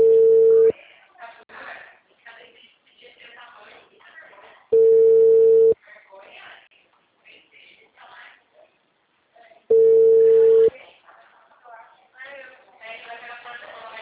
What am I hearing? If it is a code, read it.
Ringback tone (China, 450 Hz), with speech.
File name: out- -2010